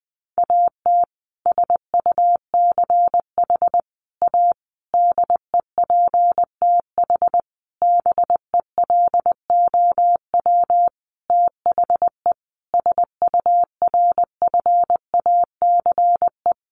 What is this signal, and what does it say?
Morse code, heard as a keyed tone, about 20 wpm, 700 Hz. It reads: AT SUCH A DEPTH BELOW THE SURFACE
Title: Sentences from Top 1200 Words 20wpm